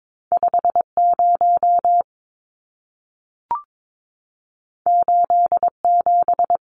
Morse code, heard as a keyed tone, about 22 wpm, 700 Hz. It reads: 50 87
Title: Numbers - 2-Digit 22wpm